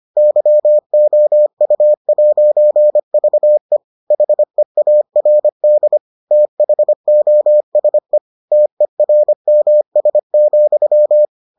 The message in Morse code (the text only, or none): YOU'VE HEARD THOSE TERMS,